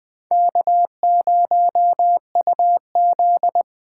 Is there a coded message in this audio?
Yes: Morse code from 0.3 to 3.6 s, heard as a keyed tone, about 22 wpm, 700 Hz. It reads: K0UZ